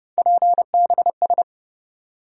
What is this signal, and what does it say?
Morse code, heard as a keyed tone, about 31 wpm, 700 Hz. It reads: PBS
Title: USA Call Sign Postfixes - 3 Letters 30wpm